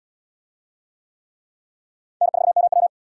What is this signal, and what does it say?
Morse code, heard as a keyed tone, about 72 wpm, 700 Hz. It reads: N4MW